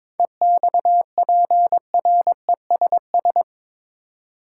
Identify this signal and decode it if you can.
Morse code, heard as a keyed tone, about 22 wpm, 700 Hz. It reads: EXPRESS
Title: Seven Letter Words - 22wpm